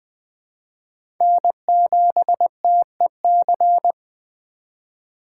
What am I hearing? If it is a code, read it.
Morse code, heard as a keyed tone, about 21 wpm, 700 Hz. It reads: N7TEC
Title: USA Call Signs - 8x Character Spacing - 20wpm